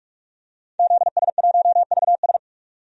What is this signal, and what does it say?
Morse code, heard as a keyed tone, about 44 wpm, 700 Hz. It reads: ZS1VS
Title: DXCC-100 Call Signs - 5 Characters - 45wpm